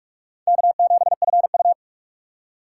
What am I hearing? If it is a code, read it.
Morse code, heard as a keyed tone, about 46 wpm, 700 Hz. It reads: K7FU
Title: USA Call Signs - 4 Characters - 6x Character Spacing - 45wpm